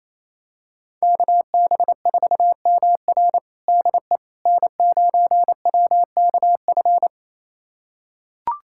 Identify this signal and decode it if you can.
Morse code, heard as a keyed tone, about 28 wpm, 700 Hz. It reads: KB4MR DE N9WKF
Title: USA Callsign DE Callsign - 5 Characters - 28wpm